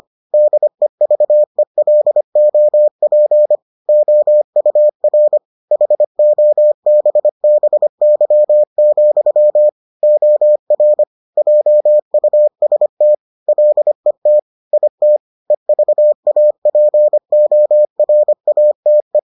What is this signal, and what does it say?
Morse code, heard as a keyed tone, about 25 wpm, 600 Hz. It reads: DEVELOP OUR HOBBY, OR JUST LET IT EVAPORATE